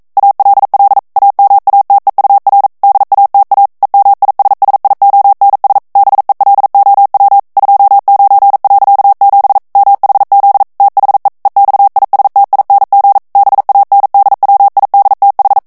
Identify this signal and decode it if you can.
Morse code, heard as a keyed tone, about 42 wpm, 800 Hz. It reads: ALL AMATEUR DATA EMISSIONS BELOW 29.7 MHZ THE EXISTING BANDWIDTH